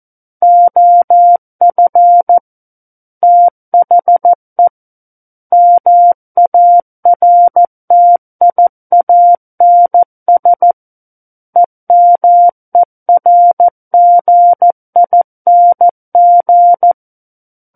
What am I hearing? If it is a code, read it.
Morse code, heard as a keyed tone, about 14 wpm, 700 Hz. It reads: OF THE MARTIANS EMERGING
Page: War of the Worlds - 05-Chapter 5 - 14 WPM